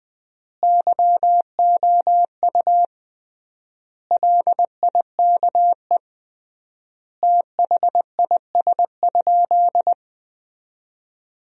Morse code, heard as a keyed tone, about 20 wpm, 700 Hz. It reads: YOU LIKE THIS?
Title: 3x Word-Spacing - Sentences from Top 200 Words - 20wpm